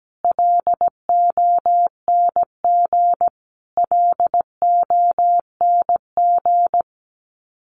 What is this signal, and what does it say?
Morse code, heard as a keyed tone, about 17 wpm, 700 Hz. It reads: LONG LONG
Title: Top 300 Words Repeated 17wpm